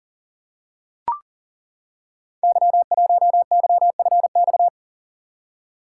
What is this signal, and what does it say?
Morse code, heard as a keyed tone, about 39 wpm, 690 Hz. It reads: Y1YFX
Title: Code Groups - 5 Letter:Numbers - 40wpm